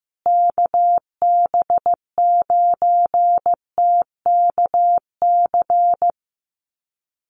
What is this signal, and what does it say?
Morse code, heard as a keyed tone, about 15 wpm, 700 Hz. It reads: KB9TKC
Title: USA Call Signs Repeated 2x 15wpm